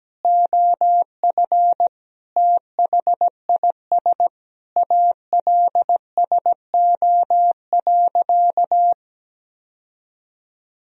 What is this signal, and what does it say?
Morse code, heard as a keyed tone, about 17 wpm, 700 Hz. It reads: OF THIS ALSO.
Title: Sentences from Top 1000 Words - Review Part 1 - 17wpm